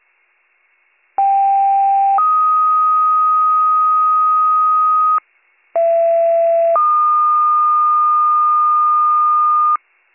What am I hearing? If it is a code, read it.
Two-tone page, heard as a one-second tone, then a three-second tone, about 790 then 1250 Hz, about 670 then 1220 Hz. It is heard from 1.2 to 9.8 s.